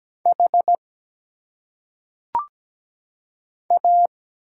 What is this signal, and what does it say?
Morse code, heard as a keyed tone, about 17 wpm, 700 Hz. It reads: H A
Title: Single Letters - Phonetic Pronunciation 17wpm